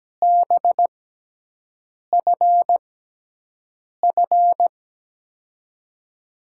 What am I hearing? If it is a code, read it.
Morse code, heard as a keyed tone, about 17 wpm, spaced as for 6 wpm, 700 Hz. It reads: BFF
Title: USA Call Signs - 5 Characters - 6x Character Spacing - 17wpm